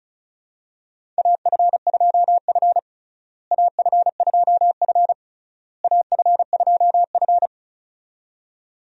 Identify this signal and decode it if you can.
Morse code, heard as a keyed tone, about 34 wpm, 700 Hz. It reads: AF2F AF2F AF2F
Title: USA Call Signs - 4 Characters - Repeated 3x - 35wpm